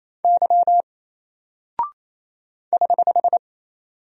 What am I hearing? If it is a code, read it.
Morse code, heard as a keyed tone, about 27 wpm, 700 Hz. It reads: Y <HH>
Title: Single Character (Letter:Number:Punctuation:Prosign) 28wpm